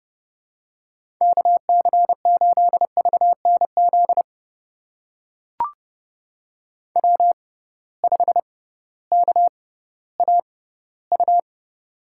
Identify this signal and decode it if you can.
Morse code, heard as a keyed tone, about 31 wpm, spaced as for 16 wpm, 700 Hz. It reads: KC8VNZ W5KAU
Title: USA Call Signs - 6x Character Spacing - 30wpm